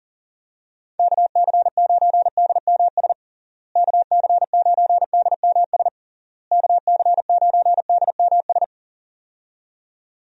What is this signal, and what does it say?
Morse code, heard as a keyed tone, about 40 wpm, 700 Hz. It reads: KC9DMS KC9DMS KC9DMS
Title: USA Call Signs - 6 Characters - Repeated 3x - 40wpm